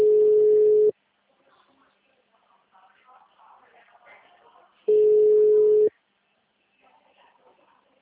Ringback tone (Europe, 425 Hz).
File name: out- -2005- - - .43790